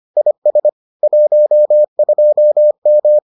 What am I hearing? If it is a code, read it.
Morse code, heard as a keyed tone, about 25 wpm, 600 Hz. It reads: IS 12M